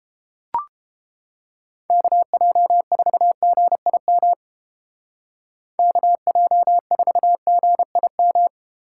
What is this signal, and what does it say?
Morse code, heard as a keyed tone, about 29 wpm, 700 Hz. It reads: KJ4GIM KJ4GIM